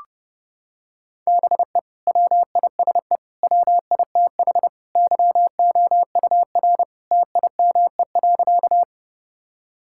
Morse code, heard as a keyed tone, about 30 wpm, 700 Hz. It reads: BE WISE WITH YOUR TIME.
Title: Sentences from Top 1600 Words 30wpm